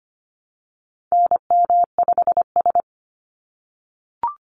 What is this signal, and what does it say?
Morse code, heard as a keyed tone, about 25 wpm, 700 Hz. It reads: NM5S
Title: NA SOTA Activator Callsigns 25wpm